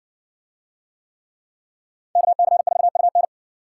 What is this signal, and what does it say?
Morse code, heard as a keyed tone, about 57 wpm, 700 Hz. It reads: KC4UN